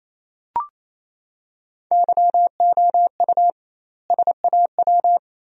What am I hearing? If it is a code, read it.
Morse code, heard as a keyed tone, about 28 wpm, 700 Hz. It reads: YOU SAW
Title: 2x Word-Spacing - Sentences from Top 300 Words - 28wpm